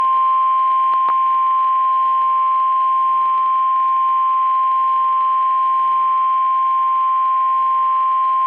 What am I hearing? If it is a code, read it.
Emergency Alert System (SAME) tones (the 1050 Hz attention tone). It is heard from 1.1 to 7.4 s.
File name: eas messages / EAS WXM31 Elmira